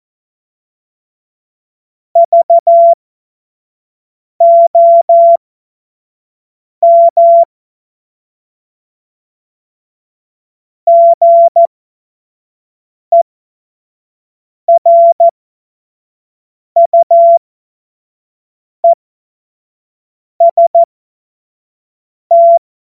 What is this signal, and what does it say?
Morse code, heard as a keyed tone, about 14 wpm, spaced as for 5 wpm, 680 Hz. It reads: VOM GERUEST
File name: Morsekurs / Witze / Witz 02 Januar 2024 25 F70